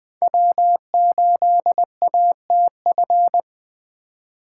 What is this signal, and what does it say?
Morse code, heard as a keyed tone, about 21 wpm, 700 Hz. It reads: W8ATF